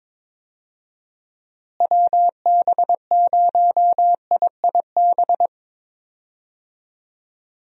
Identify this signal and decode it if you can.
Morse code, heard as a keyed tone, about 22 wpm, 700 Hz. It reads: WB0IIB
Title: USA Call Signs - 6 Characters - Repeated 3x - 22wpm